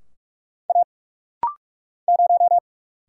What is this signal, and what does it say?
Morse code, heard as a keyed tone, about 44 wpm, 700 Hz. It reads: A 0
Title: Single Letter-Number - Rapid-Fire 45wpm